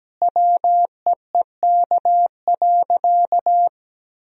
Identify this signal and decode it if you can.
Morse code, heard as a keyed tone, about 17 wpm, 700 Hz. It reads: WEEK.
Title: Sentences from Top 1000 Words - Review Part 4 - 17wpm